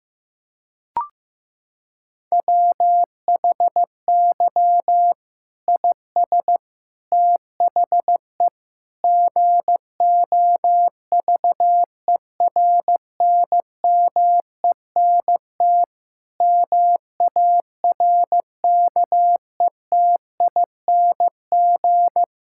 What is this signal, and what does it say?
Morse code, heard as a keyed tone, about 16 wpm, 700 Hz. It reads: WHY IS THE GOVERNMENT MARKETING
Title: Sentences from Top 1000 Words - Review Part 5 - 15wpm